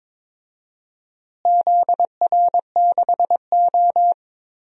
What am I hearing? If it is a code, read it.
Morse code, heard as a keyed tone, about 22 wpm, 700 Hz. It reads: ZR6O
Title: DXCC-100 Call Signs - 4 Characters - Repeated 2x - 22wpm